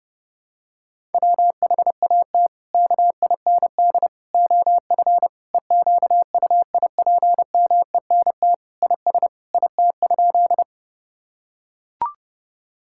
Morse code, heard as a keyed tone, about 32 wpm, 700 Hz. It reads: WHAT KIND OF EQUIPMENT IS IT?
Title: Sentences from Top 1000 Words - Review Part 5 - 30wpm